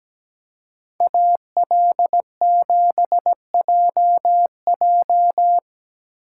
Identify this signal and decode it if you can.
Morse code, heard as a keyed tone, about 18 wpm, 700 Hz. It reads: AL7JJ
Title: Speed Racing - USA 2x2 Call Signs 17wpm